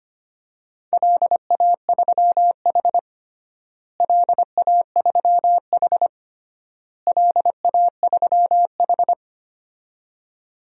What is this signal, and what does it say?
Morse code, heard as a keyed tone, about 25 wpm, 700 Hz. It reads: LA3H LA3H LA3H
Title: DXCC-100 Call Signs - 4 Characters - Repeated 3x - 25wpm